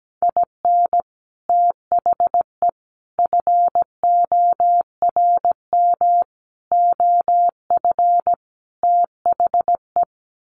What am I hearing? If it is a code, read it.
Morse code, heard as a keyed tone, about 17 wpm, 700 Hz. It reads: IN THE FORM OF THE
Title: Sets of 5 Words Encore2 17wpm